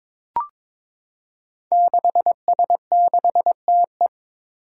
Morse code, heard as a keyed tone, about 22 wpm, 700 Hz. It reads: 6S6TE
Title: Five Challenging Letter-Number Combinations 22wpm